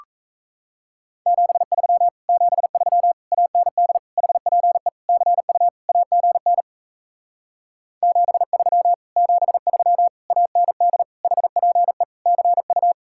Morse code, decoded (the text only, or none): 73 73 AND HPE CU AGN 73 73 AND HPE CU